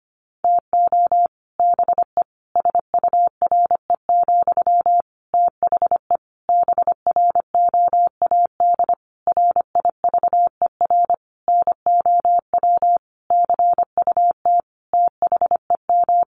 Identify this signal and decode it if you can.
Morse code, heard as a keyed tone, about 25 wpm, 700 Hz. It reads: TO BE SURE, THE BROAD RIVER NOW CUT THEM
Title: Sentences from Top 1000 Words 25wpm